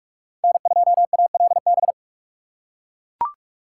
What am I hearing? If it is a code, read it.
Morse code, heard as a keyed tone, about 42 wpm, 700 Hz. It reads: N2ALB